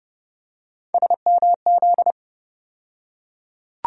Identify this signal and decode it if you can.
Morse code, heard as a keyed tone, about 30 wpm, 700 Hz. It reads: SMZ
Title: USA Call Sign Postfixes - 3 Letters 30wpm